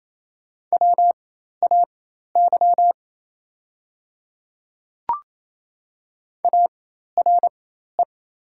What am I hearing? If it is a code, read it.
Morse code, heard as a keyed tone, about 27 wpm, spaced as for 13 wpm, 700 Hz. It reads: WAY ARE
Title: 4x Character Spacing - Top 100 Words - 28wpm